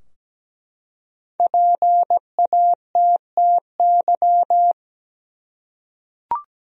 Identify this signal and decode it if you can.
Morse code, heard as a keyed tone, about 17 wpm, 700 Hz. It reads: PATTY
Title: USA Common Names - 17wpm